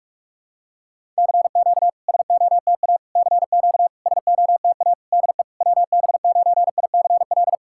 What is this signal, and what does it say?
Morse code, heard as a keyed tone, about 48 wpm, 700 Hz. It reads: CQ SOTA CQ SOTA DE WB9ICL
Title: SOTA QSO Practice 45wpm